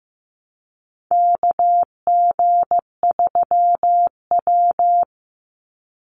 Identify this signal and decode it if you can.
Morse code, heard as a keyed tone, about 15 wpm, 700 Hz. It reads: KG3W
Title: NA SOTA Chaser Callsigns 15wpm